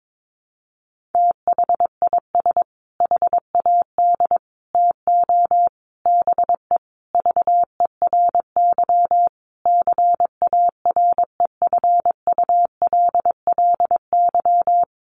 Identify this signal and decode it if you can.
Morse code, heard as a keyed tone, about 22 wpm, 700 Hz. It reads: THIS HAD TO BE VERY CAREFULLY
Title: Sentences from Top 1300 Words 22wpm